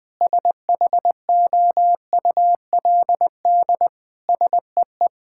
Morse code, heard as a keyed tone, about 21 wpm, 700 Hz. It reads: SHOULD SEE
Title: Sentences from Top 1600 Words 20wpm